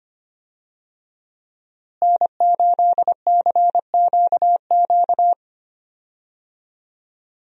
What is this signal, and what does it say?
Morse code, heard as a keyed tone, about 25 wpm, 700 Hz. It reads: N8CQQ
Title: USA Call Signs - 5 Characters - Repeated 3x - 25wpm